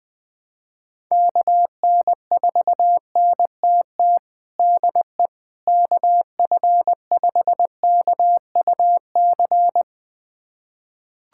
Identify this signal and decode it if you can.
Morse code, heard as a keyed tone, about 20 wpm, 700 Hz. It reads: KN4NTT DE KF5KUC
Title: USA Callsign DE Callsign 20wpm